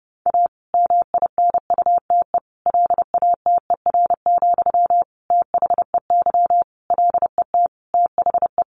Morse code, heard as a keyed tone, about 30 wpm, 700 Hz. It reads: A MINUTE LATER, THEY LET THE
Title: Sentences from Top 1500 Words 30wpm